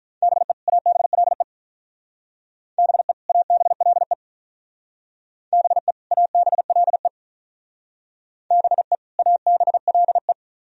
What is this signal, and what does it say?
Morse code, heard as a keyed tone, about 42 wpm, 700 Hz. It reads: BE ABLE BE ABLE BE ABLE BE ABLE
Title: Speed-Racing - Sets of 2 Words 35wpm